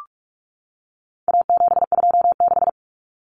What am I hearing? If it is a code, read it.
Morse code, heard as a keyed tone, about 44 wpm, 700 Hz. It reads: A726
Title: Common Cut Numbers - 4-Digit 45wpm